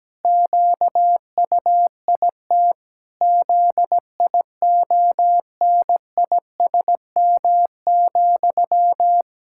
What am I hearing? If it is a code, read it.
Morse code, heard as a keyed tone, about 17 wpm, 700 Hz. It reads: QUIT ZIONISM,
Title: Pangram Challenge v2 - 17wpm